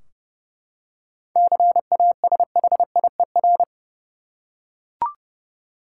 Morse code, heard as a keyed tone, about 30 wpm, 700 Hz. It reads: CASHIER
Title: Practice QSO - Exchange 3 - More Info - 30wpm